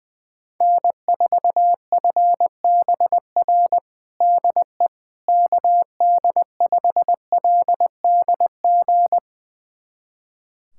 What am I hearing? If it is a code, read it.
Morse code, heard as a keyed tone, about 20 wpm, 700 Hz. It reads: N4FBR DE KD5LDG
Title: USA Callsign DE Callsign 20wpm